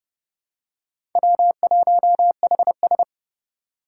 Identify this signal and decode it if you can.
Morse code, heard as a keyed tone, about 30 wpm, 700 Hz. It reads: W1HS
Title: USA Call Signs - 3x Character Spacing - 30wpm